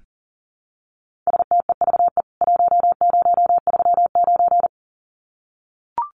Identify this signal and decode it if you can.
Morse code, heard as a keyed tone, about 40 wpm, 700 Hz. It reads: STEVE 1039
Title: CWT Exchange 40wpm